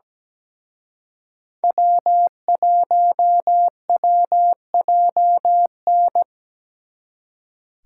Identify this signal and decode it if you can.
Morse code, heard as a keyed tone, about 20 wpm, 700 Hz. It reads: W1WJN
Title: Speed Racing - USA 1x3 Call Signs 17wpm